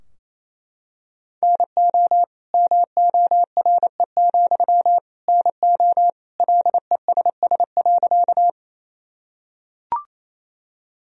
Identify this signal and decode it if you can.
Morse code, heard as a keyed tone, about 28 wpm, 700 Hz. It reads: NO MORE, NO LESS.
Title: Sentences from Top 1000 Words - Review Part 2 - 28wpm